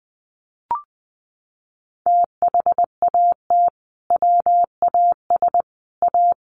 Morse code, heard as a keyed tone, about 19 wpm, 700 Hz. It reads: THAT WAS A
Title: Sentences from Top 800 Words 20wpm